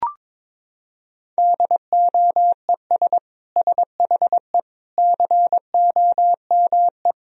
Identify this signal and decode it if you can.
Morse code, heard as a keyed tone, about 22 wpm, 700 Hz. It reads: DOES SHE COME
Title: Sentences from Top 1000 Words - Review Part 4 - 22wpm